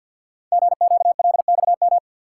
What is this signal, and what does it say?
Morse code, heard as a keyed tone, about 52 wpm, 700 Hz. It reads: GQLXM
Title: Code Groups - 5 Letters - 50wpm